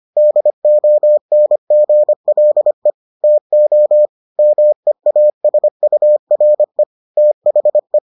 Morse code, heard as a keyed tone, about 25 wpm, 600 Hz. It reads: DONGLE TO MEASURE THE